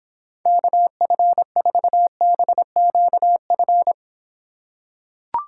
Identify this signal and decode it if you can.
Morse code, heard as a keyed tone, about 26 wpm, 700 Hz. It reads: KF4BQF